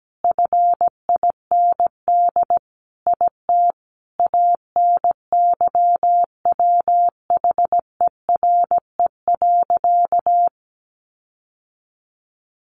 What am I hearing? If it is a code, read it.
Morse code, heard as a keyed tone, about 17 wpm, 700 Hz. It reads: FIND IT ANYWHERE.
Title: Sentences from Top 1400 Words 17wpm